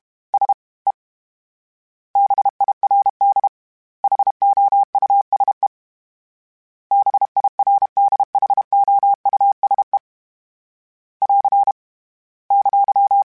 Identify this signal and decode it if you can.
Morse code, heard as a keyed tone, about 32 wpm, 800 Hz. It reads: S E BIRD HOUSE BIRDHOUSE <AR> !
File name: General / General-mp3s / Houses @30